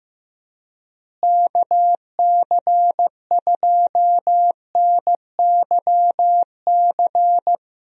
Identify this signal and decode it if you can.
Morse code, heard as a keyed tone, about 15 wpm, 700 Hz. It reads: KC2NYC
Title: USA Call Signs - 6 Characters - Repeated 3x - 15wpm